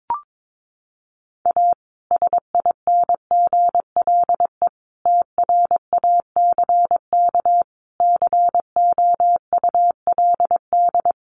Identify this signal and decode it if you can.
Morse code, heard as a keyed tone, about 22 wpm, 700 Hz. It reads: A SINGLE TRACK COULD
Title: Sentences from Top 1100 Words 22wpm